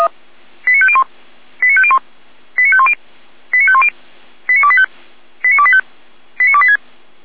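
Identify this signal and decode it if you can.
Five-tone selective call (ZVEI), heard as quick runs of five tones, tones of about 80 ms. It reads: 89501 89501 89510 89510 89285 89285 89286
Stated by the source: ZVEI five-tone selective call 89223 received by SDR.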